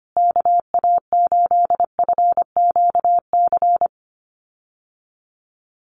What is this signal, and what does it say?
Morse code, heard as a keyed tone, about 25 wpm, 700 Hz. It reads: KA8FQC
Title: USA Call Signs Repeated 3x 25wpm